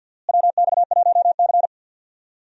Morse code, heard as a keyed tone, about 53 wpm, 700 Hz. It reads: WX1C
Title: USA Call Signs - 4 Characters - Repeated 2x - 50wpm